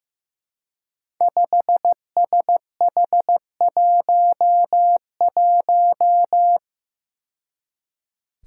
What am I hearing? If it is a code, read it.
Morse code, heard as a keyed tone, about 15 wpm, 700 Hz. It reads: 5SH11
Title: Five Challenging Letter-Number Combinations 15wpm